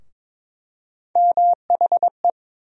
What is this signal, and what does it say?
Morse code, heard as a keyed tone, about 22 wpm, 700 Hz. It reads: MHE
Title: USA Call Sign Postfixes - 3 Letters 22wpm